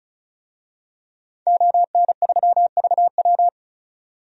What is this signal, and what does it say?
Morse code, heard as a keyed tone, about 34 wpm, 700 Hz. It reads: ON3VW
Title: DXCC-100 Call Signs - 5 Characters - 35wpm